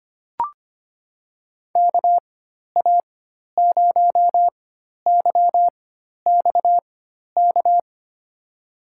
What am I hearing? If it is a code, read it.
Morse code, heard as a keyed tone, about 25 wpm, spaced as for 12 wpm, 700 Hz. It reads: KA0YXK
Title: USA Call Signs - 6 Characters - 4x Character Spacing - 25wpm